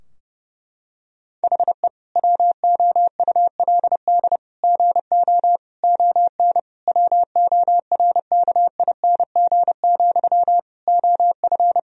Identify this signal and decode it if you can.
Morse code, heard as a keyed tone, about 30 wpm, 700 Hz. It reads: HE WOULD GO ON WORKING, OF
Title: Sentences from Top 1000 Words - Review Part 4 - 30wpm